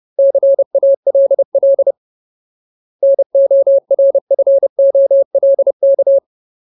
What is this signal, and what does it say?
Morse code, heard as a keyed tone, about 30 wpm, 550 Hz. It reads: CALL NORFOLK
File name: Ultimate Word List / 30WPM Ultimate